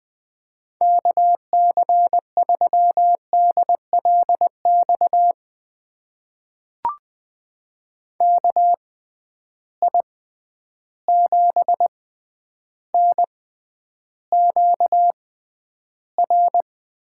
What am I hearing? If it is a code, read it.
Morse code, heard as a keyed tone, about 21 wpm, spaced as for 10 wpm, 700 Hz. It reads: KC3DLX KI7NQR
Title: USA Call Signs - 6 Characters - 6x Character Spacing - 20wpm